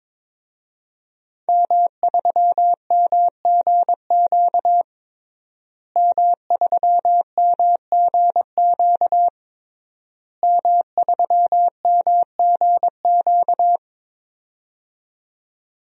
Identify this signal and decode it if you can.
Morse code, heard as a keyed tone, about 22 wpm, 700 Hz. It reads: M3MGQ M3MGQ M3MGQ
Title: DXCC-100 Call Signs - 5 Characters - Repeated 3x - 22wpm